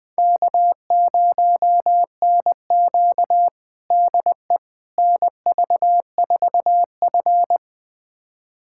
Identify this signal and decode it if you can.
Morse code, heard as a keyed tone, about 20 wpm, 700 Hz. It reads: K0NQ DE NV4F
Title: USA Callsign DE Callsign - 4 Characters - 20wpm